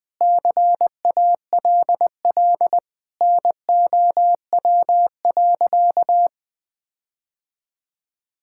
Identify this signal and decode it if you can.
Morse code, heard as a keyed tone, about 20 wpm, 700 Hz. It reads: CALL NOW.
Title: Sentences with 4-Word Phrases - 20wpm